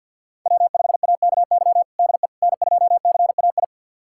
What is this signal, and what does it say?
Morse code, heard as a keyed tone, about 52 wpm, 690 Hz. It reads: W5AKY DE N2CAI
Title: USA Callsign DE Callsign - 5 Characters - 50wpm